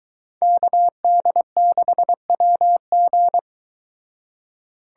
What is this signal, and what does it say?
Morse code, heard as a keyed tone, about 22 wpm, 700 Hz. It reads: KD6WG